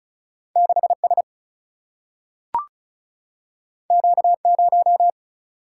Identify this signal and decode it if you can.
Morse code, heard as a keyed tone, about 36 wpm, 700 Hz. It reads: 6 S Q 0
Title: Single-Letter Single-Number 35wpm